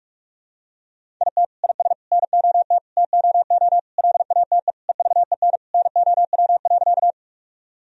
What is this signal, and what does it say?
Morse code, heard as a keyed tone, about 46 wpm, 700 Hz. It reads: IT IS NOT TOO LATE EVEN NOW.
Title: Sentences from Top 1000 Words - Review Part 1 - 45wpm